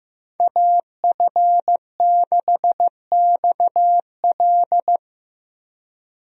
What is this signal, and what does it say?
Morse code, heard as a keyed tone, about 15 wpm, 700 Hz. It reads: AF6XL
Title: USA Call Signs - 5 Characters - Repeated 3x - 15wpm